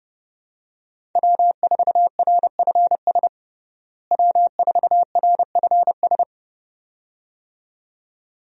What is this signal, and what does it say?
Morse code, heard as a keyed tone, about 29 wpm, 700 Hz. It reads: W4RFS W4RFS
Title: USA Call Signs - 5 Characters - Repeated 2x - 30wpm